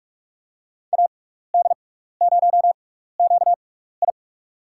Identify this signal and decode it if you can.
Morse code, heard as a keyed tone, about 44 wpm, spaced as for 15 wpm, 700 Hz. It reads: AD0QI
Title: USA Call Signs - 5 Characters - 6x Character Spacing - 45wpm